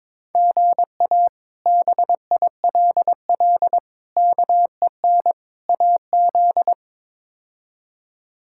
Morse code, heard as a keyed tone, about 22 wpm, 700 Hz. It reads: GA BILL KEN AZ
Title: K1USN SST QSO Practice - 22wpm